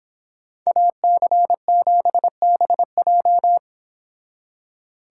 Morse code, heard as a keyed tone, about 26 wpm, 700 Hz. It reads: AC7BJ